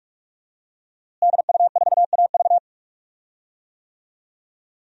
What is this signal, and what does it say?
Morse code, heard as a keyed tone, about 46 wpm, 700 Hz. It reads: DU4AV
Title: DXCC-100 Call Signs - 5 Characters - Repeated 2x - 45wpm